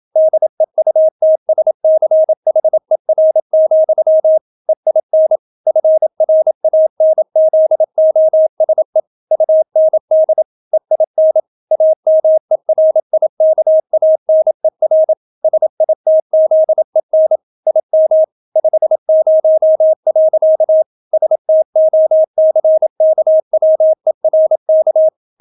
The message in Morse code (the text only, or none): DEUTSCHER, EIN FRANZOSE UND EIN AMERIKANER SITZEN IM 50. STOCKWERK